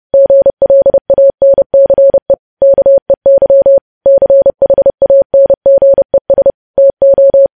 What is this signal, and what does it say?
Morse code, heard as a keyed tone, about 30 wpm, 550 Hz. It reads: GLANCE KEY CHANGES TO
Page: news headlines in morse code at 30 wpm